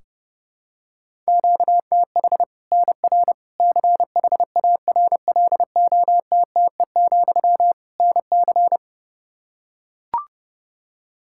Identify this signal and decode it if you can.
Morse code, heard as a keyed tone, about 30 wpm, 700 Hz. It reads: QTH NR CHARLOTTE, NC
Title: Practice QSO - Exchange 1 - Beginning - 30wpm